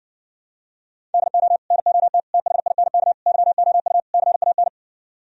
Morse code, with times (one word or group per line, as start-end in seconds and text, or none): 1.1-1.6 DO
1.7-2.2 NOT
2.3-3.1 THINK
3.3-4.0 YOU
4.1-4.7 CAN